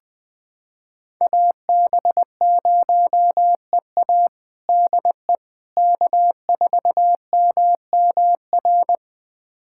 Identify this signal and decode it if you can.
Morse code, heard as a keyed tone, about 20 wpm, 700 Hz. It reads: AB0EA DE K4MMR
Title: USA Callsign DE Callsign - 5 Characters - 20wpm